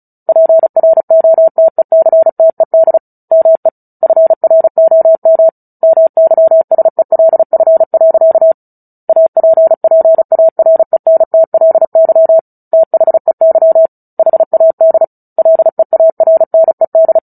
Morse code, heard as a keyed tone, about 34 wpm, 650 Hz. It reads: PROTECTED ME FROM MYSELF. APPARENTLY THEY HAD LEARNED
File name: Book Two / 09-Chapter 9 - 35 WPM